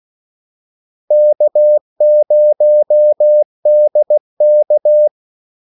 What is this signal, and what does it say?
Morse code, heard as a keyed tone, about 16 wpm, 600 Hz. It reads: K0DK
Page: Kaldesignaler 16 wpm